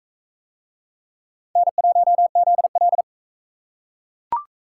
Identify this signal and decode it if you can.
Morse code, heard as a keyed tone, about 40 wpm, 700 Hz. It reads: N1ZL